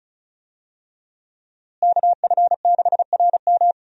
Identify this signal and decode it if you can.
Morse code, heard as a keyed tone, about 34 wpm, 700 Hz. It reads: KF6RM
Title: USA Call Signs - 5 Characters - Repeated 3x - 35wpm